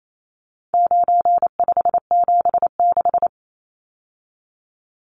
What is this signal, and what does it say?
Morse code, heard as a keyed tone, about 28 wpm, 700 Hz. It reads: 9576
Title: Numbers - 4-Digit 28wpm